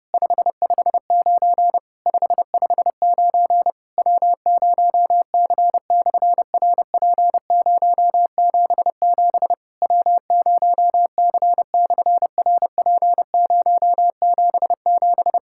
Morse code, heard as a keyed tone, about 30 wpm, 700 Hz. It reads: 559 559 W0C/RP077 W0C/RP077
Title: SOTA QSO Practice 30wpm